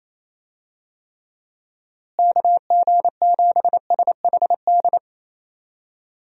Morse code, heard as a keyed tone, about 27 wpm, 700 Hz. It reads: KG7SHD
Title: USA Call Signs - 6 Characters - Repeated 2x - 28wpm